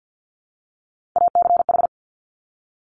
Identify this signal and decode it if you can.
Morse code, heard as a keyed tone, about 50 wpm, 700 Hz. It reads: ACH
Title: Top Three Letter Combinations in Words 50wpm